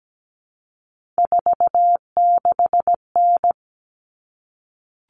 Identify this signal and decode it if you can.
Morse code, heard as a keyed tone, about 17 wpm, 700 Hz. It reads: 46N
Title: All Cut Numbers - 3-Digit 17wpm